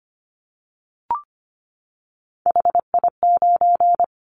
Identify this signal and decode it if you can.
Morse code, heard as a keyed tone, about 25 wpm, 700 Hz. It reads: HI9
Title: Callsign Prefixes for Top 100 DXCC - 25wpm